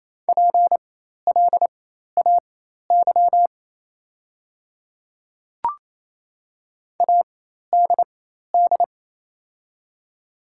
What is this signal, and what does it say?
Morse code, heard as a keyed tone, about 28 wpm, spaced as for 13 wpm, 700 Hz. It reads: PLAY ADD
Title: 4x Character Spacing - Top 300 Words - 28wpm